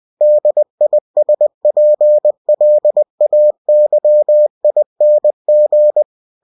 Morse code, heard as a keyed tone, about 20 wpm, 600 Hz. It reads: DISPLAYING
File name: cw-books / winnie-the-pooh / winnie-20wpm-page-0114